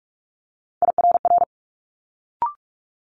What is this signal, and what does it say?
Morse code, heard as a keyed tone, about 46 wpm, 700 Hz. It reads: IRR
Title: Multi-letter Phonemes Quick Fire 45wpm